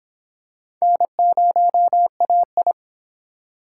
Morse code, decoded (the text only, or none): N0AI